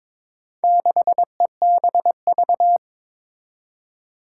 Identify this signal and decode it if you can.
Morse code, heard as a keyed tone, about 22 wpm, 700 Hz. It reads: 6EBV
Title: Four Challenging Letter-Number Combinations 22wpm